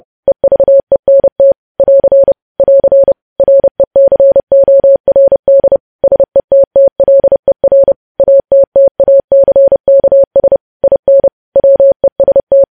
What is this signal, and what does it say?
Morse code, heard as a keyed tone, about 30 wpm, 550 Hz. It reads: EVENT <AR> <AR> RECORD SETTLER ATTACKS IN WEST
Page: News Headlines in Morse Code at 30 WPM